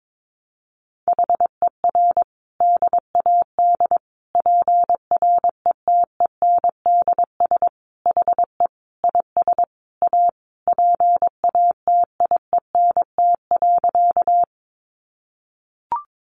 Morse code, heard as a keyed tone, about 22 wpm, 700 Hz. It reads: HER DAD PRETENDS HE IS A PATIENT.
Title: Additional Easy Phrases 22wpm